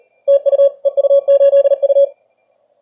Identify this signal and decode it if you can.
Morse code, heard as a keyed tone, about 37 wpm, 560 Hz. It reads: TU EU8U
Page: Сегодня решил проверить динамический диапазон приемника в экстримальных условиях теста. Плотность станций на диапазонах большая.